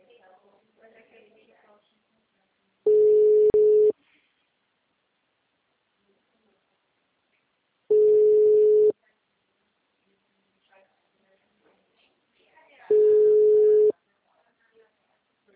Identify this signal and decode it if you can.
Ringback tone (Europe, 425 Hz), with speech.